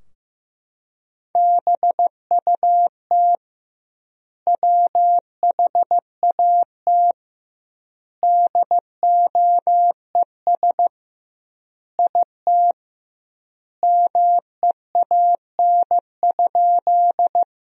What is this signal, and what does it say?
Morse code, heard as a keyed tone, about 16 wpm, 700 Hz. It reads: BUT WHAT DOES IT MEAN?
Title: 2x Word-Spacing - Sentences from Top 200 Words - 15wpm